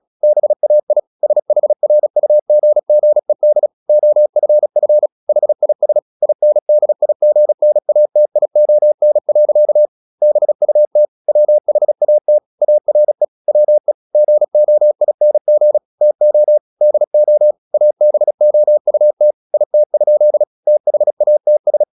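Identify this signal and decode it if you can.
Morse code, heard as a keyed tone, about 36 wpm, 600 Hz. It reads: BAI SHRUGGED OFF HIS INDIGNATION. BUT WHAT ARE WE GOING TO DO ABOUT IT? THATS
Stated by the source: Morse practice at 36 wpm, text of Ken Follett's novel Never, from CW med Gnister.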